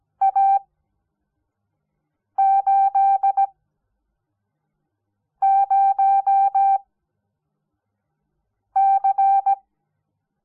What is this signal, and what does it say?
Morse code, heard as a keyed tone, about 17 wpm, spaced as for 4 wpm, 780 Hz. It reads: A80C
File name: Onlinelehrgang / morsen / DJ4UF Morsen06